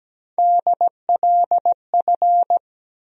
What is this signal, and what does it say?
Morse code, heard as a keyed tone, about 17 wpm, 700 Hz. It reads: DLF
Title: USA Call Sign Postfixes - 3 Letters 17wpm